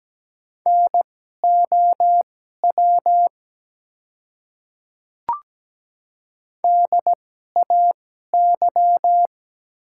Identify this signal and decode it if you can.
Morse code, heard as a keyed tone, about 17 wpm, spaced as for 12 wpm, 700 Hz. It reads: NOW DAY
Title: 2x Character Spacing - Top 200 Words - 17wpm